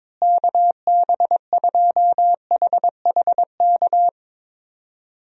Morse code, heard as a keyed tone, about 22 wpm, 700 Hz. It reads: KB2HHK
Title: USA Call Signs - 6 Characters - Repeated 2x - 22wpm